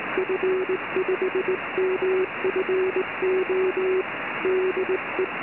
Morse code, heard as a keyed tone, about 19 wpm, 380 Hz. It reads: F5MFO DE